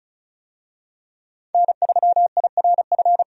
Morse code, heard as a keyed tone, about 36 wpm, 700 Hz. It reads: N3IRF
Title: USA Call Signs - 5 Characters - 35wpm